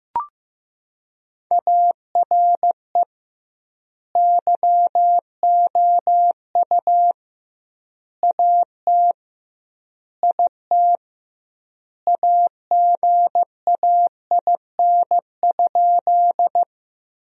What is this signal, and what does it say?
Morse code, heard as a keyed tone, about 16 wpm, 700 Hz. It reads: ARE YOU AT IT AGAIN?
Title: 2x Word-Spacing - Sentences from Top 200 Words - 15wpm